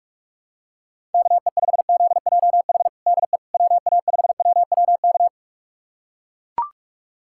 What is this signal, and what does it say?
Morse code, heard as a keyed tone, about 42 wpm, 700 Hz. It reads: KE5ZJH DE WA5WWK
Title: USA Callsign DE Callsign - 6 Characters - 45wpm